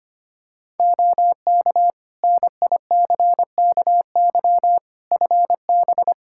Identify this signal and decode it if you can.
Morse code, heard as a keyed tone, about 25 wpm, 700 Hz. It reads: OK NICKY FB
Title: Practice QSO - Exchange 2 - Info - 25wpm